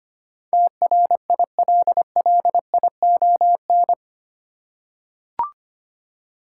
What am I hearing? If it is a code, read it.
Morse code, heard as a keyed tone, about 25 wpm, 700 Hz. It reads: TRILLION
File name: Eight Letter Words - 25wpm